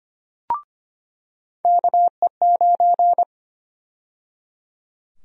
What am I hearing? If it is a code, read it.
Morse code, heard as a keyed tone, about 25 wpm, 700 Hz. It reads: KE9
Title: USA Call Sign Prefixes - 25wpm